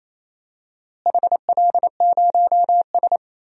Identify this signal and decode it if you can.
Morse code, heard as a keyed tone, about 27 wpm, 700 Hz. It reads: HL0S
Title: DXCC-100 Call Signs - 4 Characters - Repeated 3x - 28wpm